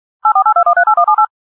DTMF, heard as telephone touch tones, tones of about 100 ms. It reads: 8482138178
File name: issues / dtmf dialing